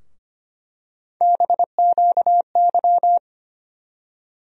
Morse code, heard as a keyed tone, about 25 wpm, 700 Hz. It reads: BQY
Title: USA Call Sign Postfixes - 3 Letters 25wpm